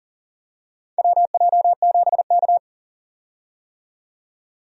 Morse code, heard as a keyed tone, about 39 wpm, 700 Hz. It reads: WJ7K
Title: USA Call Signs - 4 Characters - 40wpm